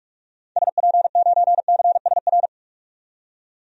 Morse code, heard as a keyed tone, about 47 wpm, 700 Hz. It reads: SP9CSR
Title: DXCC-100 Call Signs - 6 Characters - 45wpm